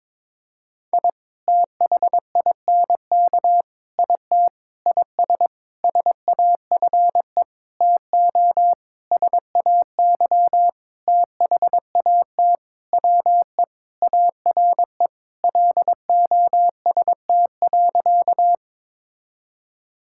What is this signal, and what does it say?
Morse code, heard as a keyed tone, about 22 wpm, 700 Hz. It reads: I THINK IT IS SAFE TO SAY THAT WE ARE LOST.
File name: Sentences with 7-Word Phrases - 22wpm